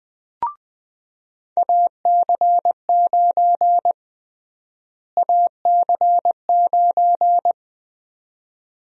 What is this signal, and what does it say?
Morse code, heard as a keyed tone, about 21 wpm, 700 Hz. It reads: AC9 AC9
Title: USA Call Sign Prefixes - Repeated 2x - 20wpm